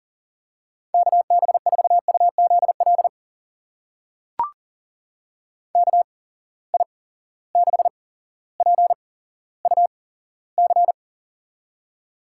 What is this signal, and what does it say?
Morse code, heard as a keyed tone, about 39 wpm, spaced as for 17 wpm, 700 Hz. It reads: KB4UZL KI6PUC
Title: USA Call Signs - 6 Characters - 8x Character Spacing - 40wpm